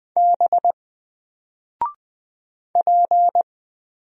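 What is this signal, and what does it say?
Morse code, heard as a keyed tone, about 20 wpm, 700 Hz. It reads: B P
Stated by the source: Single letter and number Morse drill at 20 wpm, from N4LCD.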